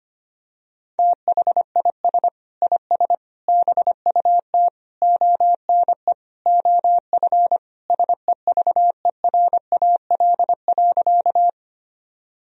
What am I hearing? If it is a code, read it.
Morse code, heard as a keyed tone, about 25 wpm, 700 Hz. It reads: THIS IS BUT ONE OF SEVERAL.
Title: Sentences from Top 1000 Words - Review Part 2 - 25wpm